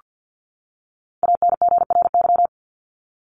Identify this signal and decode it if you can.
Morse code, heard as a keyed tone, about 48 wpm, 700 Hz. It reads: ANGRY
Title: Top 1751-2000 Words 50wpm